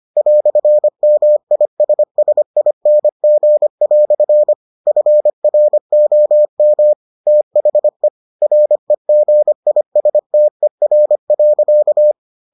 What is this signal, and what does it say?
Morse code, heard as a keyed tone, about 25 wpm, 600 Hz. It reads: "MISSING" FROM THE REGISTER.